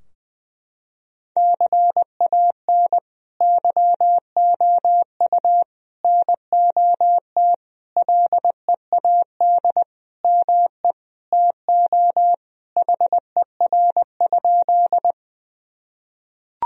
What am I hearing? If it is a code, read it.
Morse code, heard as a keyed tone, about 20 wpm, 700 Hz. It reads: CAN YOU NOT LEAD ME TO HER?
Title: Sentences from Top 1000 Words - Review Part 2 - 20wpm